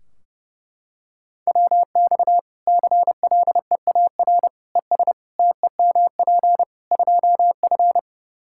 Morse code, heard as a keyed tone, about 30 wpm, 700 Hz. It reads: WX CLEAR ES TEMP 2F
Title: Practice QSO - Exchange 2 - Info - 30wpm